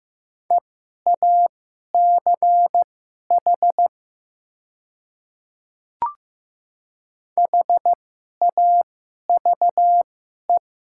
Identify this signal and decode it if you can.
Morse code, heard as a keyed tone, about 15 wpm, spaced as for 11 wpm, 700 Hz. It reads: EACH HAVE
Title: 2x Character Spacing - Top 200 Words - 15wpm